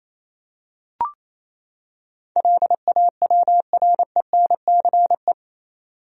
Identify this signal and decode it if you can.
Morse code, heard as a keyed tone, about 28 wpm, 700 Hz. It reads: LAWRENCE
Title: USA Common Names - 28wpm